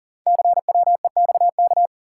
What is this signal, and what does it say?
Morse code, heard as a keyed tone, about 39 wpm, 690 Hz. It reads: CWEXK
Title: Code Groups - 5 Letters - 40wpm